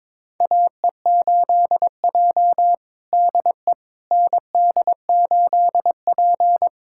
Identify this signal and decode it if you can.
Morse code, heard as a keyed tone, about 22 wpm, 700 Hz. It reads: AE8J DE ND8P
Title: USA Callsign DE Callsign - 4 Characters - 22wpm